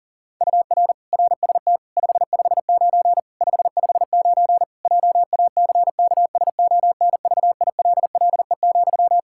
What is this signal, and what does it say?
Morse code, heard as a keyed tone, about 40 wpm, 700 Hz. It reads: UR RST 559 559 JACKSONVILLE,
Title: Practice QSO - Exchange 1 - Beginning - 40wpm